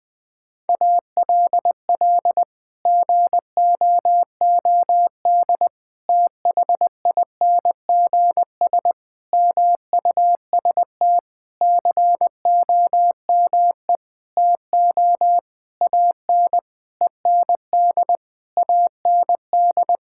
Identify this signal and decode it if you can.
Morse code, heard as a keyed tone, about 20 wpm, 700 Hz. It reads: ALL GOOD THINGS MUST COME TO AN END AND
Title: Sentences with 9-Word Phrases - Spoken Beginning - 20wpm